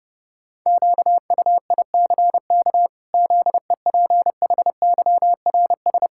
Morse code, heard as a keyed tone, about 31 wpm, 700 Hz. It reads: QUICK ZEPHYRS
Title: Pangram Challenge v2 - 30wpm